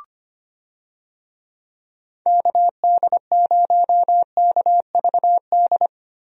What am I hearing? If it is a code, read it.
Morse code, heard as a keyed tone, about 25 wpm, 700 Hz. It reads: KD0KVD
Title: USA Call Signs - 6 Characters - Repeated 2x - 25wpm